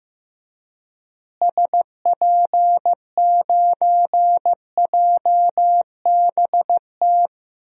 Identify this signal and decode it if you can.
Morse code, heard as a keyed tone, about 15 wpm, 700 Hz. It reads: SP9JBT
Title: DXCC-100 Call Signs - 6 Characters - Repeated 3x - 15wpm